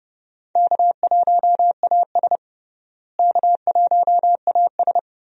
Morse code, heard as a keyed tone, about 31 wpm, 700 Hz. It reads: K1AS K1AS
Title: USA Call Signs - 4 Characters - Repeated 3x - 30wpm